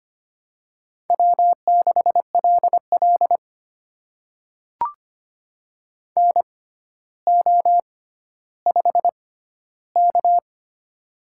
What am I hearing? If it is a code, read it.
Morse code, heard as a keyed tone, about 25 wpm, spaced as for 13 wpm, 700 Hz. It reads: W6LL NO5K
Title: USA Call Signs - 4 Characters - 6x Character Spacing - 25wpm